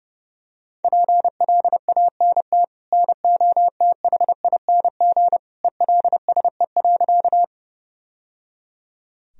Morse code, heard as a keyed tone, about 30 wpm, 700 Hz. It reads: PLANT NOTHING ELSE.
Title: Sentences from Top 1000 Words - Review Part 1 - 30wpm